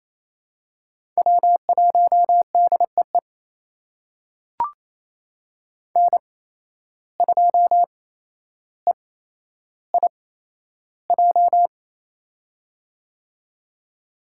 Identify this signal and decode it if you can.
Morse code, heard as a keyed tone, about 28 wpm, spaced as for 12 wpm, 700 Hz. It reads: W1DEE N2EIJ
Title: USA Call Signs - 5 Characters - 8x Character Spacing - 28wpm